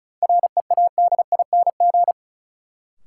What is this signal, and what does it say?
Morse code, heard as a keyed tone, about 36 wpm, 700 Hz. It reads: READING
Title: Sentences from Top 600 Words - New Word Drill - 35wpm